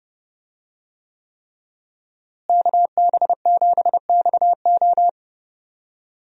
Morse code, heard as a keyed tone, about 30 wpm, 700 Hz. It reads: KB7XO
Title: USA Call Signs - 5 Characters - Repeated 3x - 30wpm